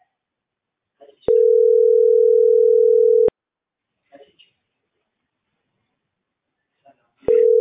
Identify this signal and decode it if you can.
Ringback tone (North America), 440 + 480 Hz.